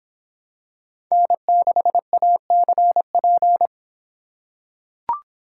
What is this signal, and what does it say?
Morse code, heard as a keyed tone, about 26 wpm, 700 Hz. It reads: N6ACP